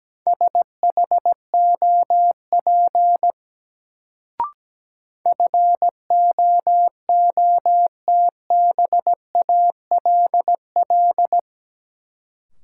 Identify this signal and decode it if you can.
Morse code, heard as a keyed tone, about 17 wpm, 700 Hz. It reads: SHOP FOOTBALL
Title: Sentences from Top 1600 Words - New Word Drill - 17wpm